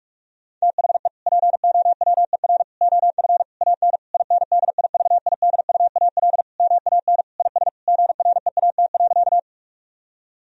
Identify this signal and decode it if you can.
Morse code, heard as a keyed tone, about 46 wpm, 700 Hz. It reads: THE POWER OF AN INDIVIDUAL MAN IS GREAT.
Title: Sentences from Top 1000 Words - Review Part 4 - 45wpm